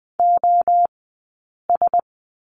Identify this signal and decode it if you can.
Morse code, heard as a keyed tone, about 20 wpm, spaced as for 8 wpm, 700 Hz. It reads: OS
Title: 034 Lesson 7d - Letter s - Two Characters 20wpm